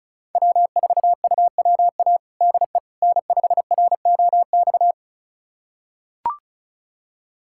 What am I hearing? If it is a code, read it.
Morse code, heard as a keyed tone, about 36 wpm, 700 Hz. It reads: W4UWA DE N5ROX
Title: USA Callsign DE Callsign - 5 Characters - 35wpm